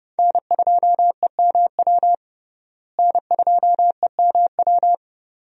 Morse code, heard as a keyed tone, about 30 wpm, 700 Hz. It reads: N2EMW N2EMW
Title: USA Call Signs - 5 Characters - Repeated 2x - 30wpm